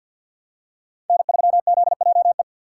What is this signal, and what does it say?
Morse code, heard as a keyed tone, about 53 wpm, 700 Hz. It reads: N3ZJE
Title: USA Call Signs - 5 Characters - 6x Character Spacing - 50wpm